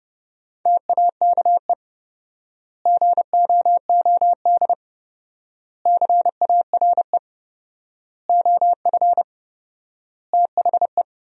Morse code, heard as a keyed tone, about 32 wpm, 700 Hz. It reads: TAKE GOOD CARE OF THE
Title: 4x Word-Spacing - Sentences from Top 400 Words - 30wpm